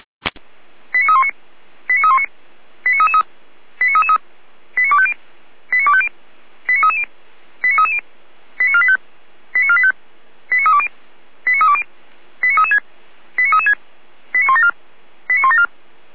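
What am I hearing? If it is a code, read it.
Five-tone selective call (ZVEI), heard as quick runs of five tones, tones of about 70 ms. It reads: 89319 89319 89333 89333 89250 89250 89330 89330 89485 89485 89320 89320 89336 89336 89174 89174
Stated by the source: ZVEI five-tone call 89140 received by SDR, Feuerwehr Orlamünde.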